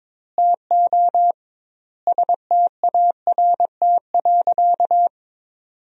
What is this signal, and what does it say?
Morse code, heard as a keyed tone, about 22 wpm, 700 Hz. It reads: TO START.
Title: 2x Word-Spacing - Sentences from Top 300 Words - 22wpm